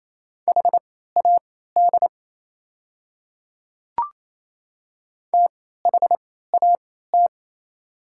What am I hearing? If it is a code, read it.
Morse code, heard as a keyed tone, about 27 wpm, spaced as for 16 wpm, 700 Hz. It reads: HAD THAT
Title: 3x Character Spacing - Top 200 Words - 28wpm